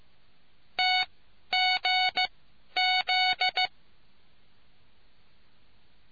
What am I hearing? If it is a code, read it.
Morse code, heard as a keyed tone, about 15 wpm, spaced as for 11 wpm, 740 Hz. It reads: TGZ